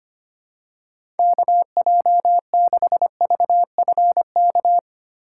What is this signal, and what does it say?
Morse code, heard as a keyed tone, about 25 wpm, 700 Hz. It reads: KJ6VFK
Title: USA Call Signs - 6 Characters - 2x Character Spacing - 25wpm